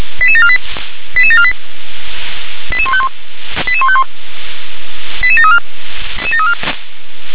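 Five-tone selective call (ZVEI), heard as quick runs of five tones, tones of about 70 ms. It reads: 88748 88748 88151 88151 88734
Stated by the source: ZVEI five-tone selective call 88749 received by SDR.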